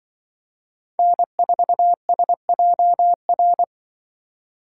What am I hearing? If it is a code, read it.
Morse code, heard as a keyed tone, about 24 wpm, 700 Hz. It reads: N4SJR
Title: Speed Racing - USA Call Signs 20wpm